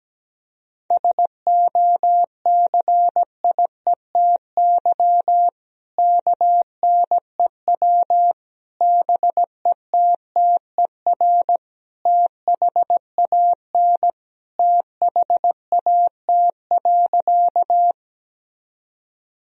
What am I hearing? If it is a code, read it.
Morse code, heard as a keyed tone, about 17 wpm, 700 Hz. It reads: SOCIETY KNEW BETTER THAN THAT.
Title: Sentences from Top 1000 Words - Review Part 4 - 17wpm